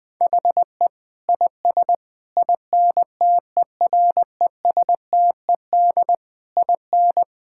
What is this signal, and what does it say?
Morse code, heard as a keyed tone, about 20 wpm, 700 Hz. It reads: HE IS INTERESTED IN
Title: Sentences from Top 1000 Words - Review Part 6 - 20wpm